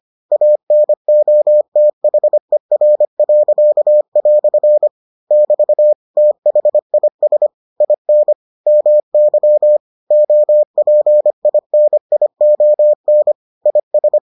Morse code, heard as a keyed tone, about 25 wpm, 600 Hz. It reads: ANOTHER." <BT> THIS IN MY OPINION IS